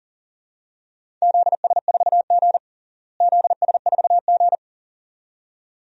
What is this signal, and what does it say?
Morse code, heard as a keyed tone, about 37 wpm, 700 Hz. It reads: ZS4G ZS4G
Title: DXCC-100 Call Signs - 4 Characters - Repeated 2x - 40wpm